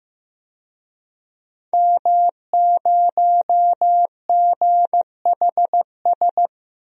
Morse code, heard as a keyed tone, about 15 wpm, 700 Hz. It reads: M0GHS
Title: DXCC-100 Call Signs - 5 Characters - Repeated 2x - 15wpm